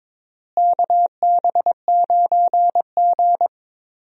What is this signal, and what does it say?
Morse code, heard as a keyed tone, about 22 wpm, 700 Hz. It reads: KB9G
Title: USA Call Signs - 4 Characters - Repeated 2x - 22wpm